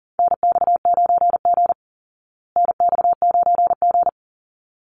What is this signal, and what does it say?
Morse code, heard as a keyed tone, about 40 wpm, 700 Hz. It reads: NX9G NX9G
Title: Call Signs Repeated 3x - 40wpm